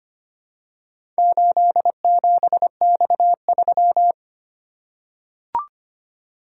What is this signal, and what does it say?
Morse code, heard as a keyed tone, about 25 wpm, 700 Hz. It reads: 87X3
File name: practice / 25wpm / Four Letter-Number Combinations - 25wpm